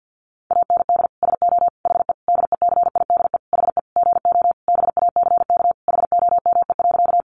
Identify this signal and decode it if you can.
Morse code, heard as a keyed tone, about 50 wpm, 700 Hz. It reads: AND SO HE DECIDE HE GO BACK HOME.
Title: Sentences from Top 500 Words 50wpm